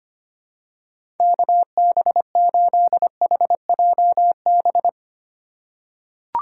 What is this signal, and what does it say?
Morse code, heard as a keyed tone, about 25 wpm, 700 Hz. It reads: KB8HJB
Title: USA Call Signs - 6 Characters - 4x Character Spacing - 25wpm